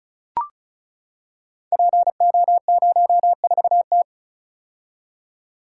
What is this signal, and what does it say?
Morse code, heard as a keyed tone, about 36 wpm, 700 Hz. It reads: PO04T
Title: Code Groups - 5 Letter:Numbers - 35wpm